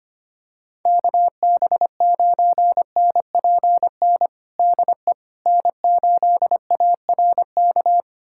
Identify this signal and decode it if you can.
Morse code, heard as a keyed tone, about 25 wpm, 700 Hz. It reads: KB9NPN DE N8ARK
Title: USA Callsign DE Callsign 25wpm